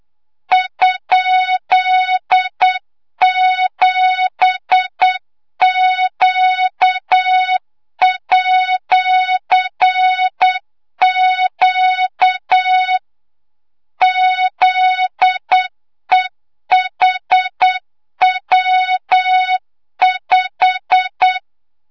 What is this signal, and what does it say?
Morse code, heard as a keyed tone, about 8 wpm, 3700 Hz, with noise. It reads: ?7Q@Q ZEHW5
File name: Morze-HA2MN / hangp / 2 06 GYAKORLAT